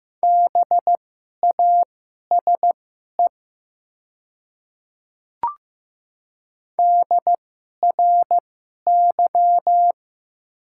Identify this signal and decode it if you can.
Morse code, heard as a keyed tone, about 15 wpm, spaced as for 11 wpm, 700 Hz. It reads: BASE DRY
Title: 2x Character Spacing - Top 500 Words - 15wpm